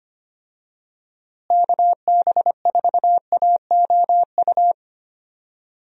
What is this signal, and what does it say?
Morse code, heard as a keyed tone, about 25 wpm, 700 Hz. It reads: KB4AOU
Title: USA Call Signs - 6 Characters - Repeated 3x - 25wpm